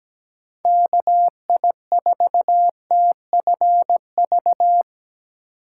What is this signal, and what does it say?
Morse code, heard as a keyed tone, about 17 wpm, 700 Hz. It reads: KI4TFV
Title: USA Call Signs - 6 Characters - 8x Character Spacing - 17wpm